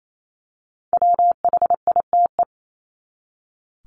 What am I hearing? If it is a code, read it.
Morse code, heard as a keyed tone, about 28 wpm, 700 Hz. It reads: WHITE
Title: Top 500 Words 28wpm